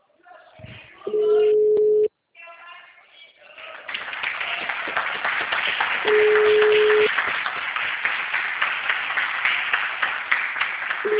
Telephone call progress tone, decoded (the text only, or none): ringback tone